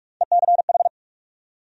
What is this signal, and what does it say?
Morse code, heard as a keyed tone, about 46 wpm, 700 Hz. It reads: ECH
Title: USA Call Sign Postfixes - 3 Letters 45wpm